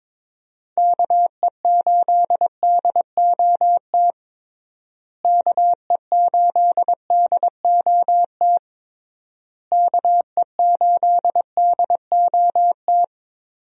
Morse code, heard as a keyed tone, about 22 wpm, 700 Hz. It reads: KE8DOT KE8DOT KE8DOT
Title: USA Call Signs - 6 Characters - Repeated 3x - 22wpm